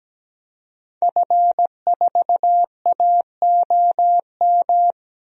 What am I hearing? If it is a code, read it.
Morse code, heard as a keyed tone, about 17 wpm, 700 Hz. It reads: F4AOM
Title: DXCC-100 Call Signs - 5 Characters - Repeated 2x - 17wpm